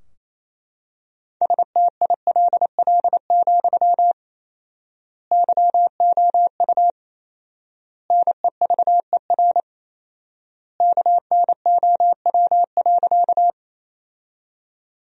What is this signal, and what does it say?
Morse code, heard as a keyed tone, about 27 wpm, 700 Hz. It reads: STILL, YOU NEVER KNOW.
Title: 4x Word-Spacing - Sentences from Top 300 Words - 28wpm